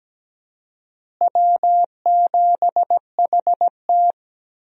Morse code, heard as a keyed tone, about 17 wpm, 700 Hz. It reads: W7HT
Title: USA Call Signs - 4 Characters - Repeated 3x - 17wpm